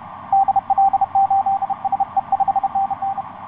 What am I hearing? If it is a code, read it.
Morse code, heard as a keyed tone, about 31 wpm, 780 Hz, with radio noise. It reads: DL8SE 5NN
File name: xv9dx ik1avw etc 12m cw 16feb2010 0909